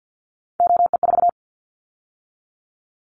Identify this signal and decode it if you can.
Morse code, heard as a keyed tone, about 50 wpm, 700 Hz. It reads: OE4
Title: Callsign Prefixes for Top 100 DXCC - 50wpm